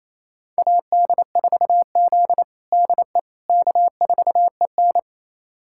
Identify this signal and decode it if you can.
Morse code, heard as a keyed tone, about 28 wpm, 700 Hz. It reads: AD4Z DE K4EN
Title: USA Callsign DE Callsign - 4 Characters - 28wpm